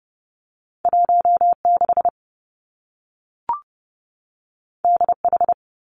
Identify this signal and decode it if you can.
Morse code, heard as a keyed tone, about 30 wpm, 700 Hz. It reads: 1 6 D H
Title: Two Challenging Letter-Number Combinations 30wpm